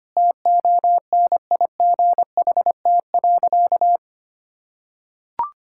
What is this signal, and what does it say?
Morse code, heard as a keyed tone, about 25 wpm, 700 Hz. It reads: TONIGHT.
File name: practice / 25wpm / Sentences with 2-Word Phrases - 25wpm